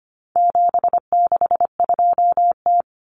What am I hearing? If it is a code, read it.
Morse code, heard as a keyed tone, about 25 wpm, 700 Hz. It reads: 762T
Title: Common Cut Numbers - 4-Digit 25wpm